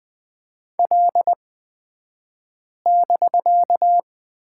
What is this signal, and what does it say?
Morse code, heard as a keyed tone, about 20 wpm, spaced as for 5 wpm, 700 Hz. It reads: L<BK>
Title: 249 Lesson 40d - BK - Two Characters 20wpm